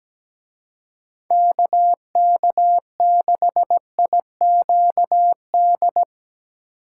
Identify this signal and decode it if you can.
Morse code, heard as a keyed tone, about 18 wpm, 700 Hz. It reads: KK6IQD
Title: Speed Racing - USA 2x3 Call Signs 17wpm